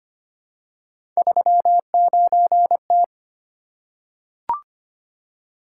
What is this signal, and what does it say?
Morse code, heard as a keyed tone, about 25 wpm, 700 Hz. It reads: 39T
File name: Three Letter-Number Combinations - 25wpm